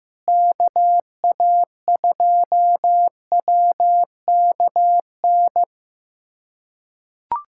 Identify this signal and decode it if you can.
Morse code, heard as a keyed tone, about 16 wpm, 700 Hz. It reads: KA2WKN
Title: USA Callsign DE Callsign 15wpm